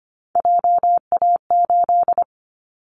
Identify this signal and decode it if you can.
Morse code, heard as a keyed tone, about 25 wpm, 700 Hz. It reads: JA8
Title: Callsign Prefixes for Top 25 DXCC - Repeated 3x - 25wpm